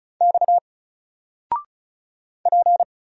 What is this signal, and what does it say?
Morse code, heard as a keyed tone, about 35 wpm, 700 Hz. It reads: X P
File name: Single-Letter-Number-35wpm